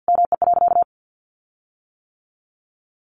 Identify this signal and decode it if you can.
Morse code, heard as a keyed tone, about 50 wpm, 700 Hz. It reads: ME.
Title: Sentences from Top 1300 Words 50wpm